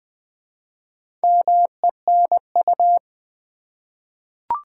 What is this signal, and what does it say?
Morse code, heard as a keyed tone, about 20 wpm, 700 Hz. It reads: MENU
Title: Easy Words with Hints - 20wpm